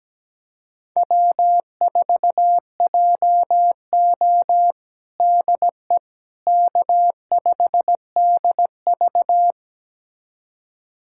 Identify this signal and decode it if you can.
Morse code, heard as a keyed tone, about 17 wpm, 700 Hz. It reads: W4JO DE K5DV
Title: USA Callsign DE Callsign - 4 Characters - 17wpm